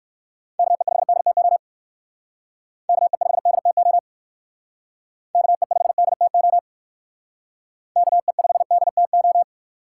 Morse code, heard as a keyed tone, about 57 wpm, 700 Hz. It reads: KE5DTO KE5DTO KE5DTO KE5DTO